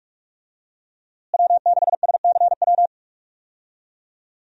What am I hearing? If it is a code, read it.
Morse code, heard as a keyed tone, about 41 wpm, 700 Hz. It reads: W6SCW
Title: USA Call Signs - 5 Characters - Repeated 2x - 45wpm